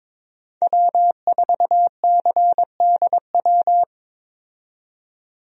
Morse code, heard as a keyed tone, about 22 wpm, 700 Hz. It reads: W4CDW
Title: USA Call Signs - 5 Characters - 22wpm